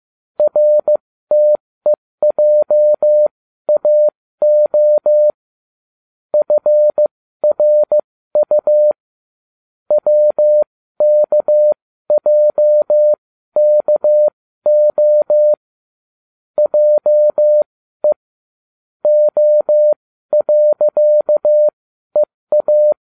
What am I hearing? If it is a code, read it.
Morse code, heard as a keyed tone, about 15 wpm, spaced as for 12 wpm, 600 Hz. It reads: RTEJAO FRU WKJKO JE O.EA